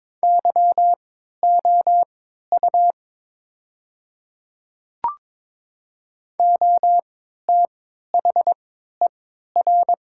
Morse code, heard as a keyed tone, about 22 wpm, spaced as for 13 wpm, 700 Hz. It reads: YOU OTHER
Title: 3x Character Spacing - Top 100 Words - 22wpm